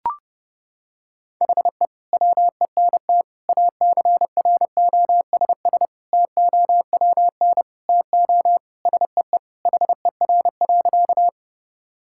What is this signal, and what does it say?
Morse code, heard as a keyed tone, about 30 wpm, 700 Hz. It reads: HE WENT ACROSS TOWN TO SEE HER.
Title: Sentences from Top 1000 Words - Review Part 5 - 30wpm